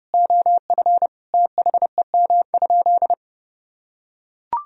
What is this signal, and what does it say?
Morse code, heard as a keyed tone, about 29 wpm, 700 Hz. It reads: OF THEM?
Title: Sentences from Top 1000 Words - Review Part 4 - 30wpm